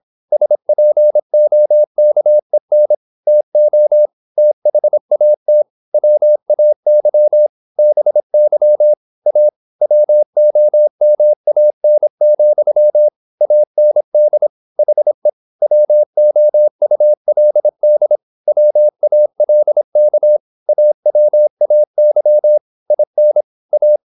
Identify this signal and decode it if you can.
Morse code, heard as a keyed tone, about 26 wpm, 600 Hz. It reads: SPOKEN TO THAT WAY BY A WOMAN, AND HE WOULD WALK AWAY IN A